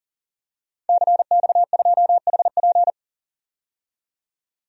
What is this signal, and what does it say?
Morse code, heard as a keyed tone, about 40 wpm, 700 Hz. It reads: CX2HP
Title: DXCC-100 Call Signs - 5 Characters - 40wpm